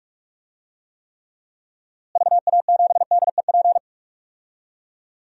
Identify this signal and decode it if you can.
Morse code, heard as a keyed tone, about 46 wpm, 700 Hz. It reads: VA7DEP
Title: DXCC-100 Call Signs - 6 Characters - Repeated 3x - 45wpm